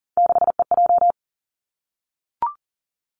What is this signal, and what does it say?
Morse code, heard as a keyed tone, about 38 wpm, 690 Hz. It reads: 6EJ
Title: Three Challenging Letter-Number Combinations 40wpm